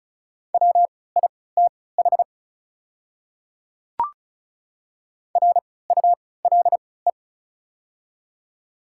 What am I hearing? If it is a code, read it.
Morse code, heard as a keyed tone, about 35 wpm, spaced as for 20 wpm, 700 Hz. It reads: WITH RULE
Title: 3x Character Spacing - Top 500 Words - 35wpm